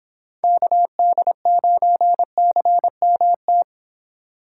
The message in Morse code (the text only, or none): KD9CMT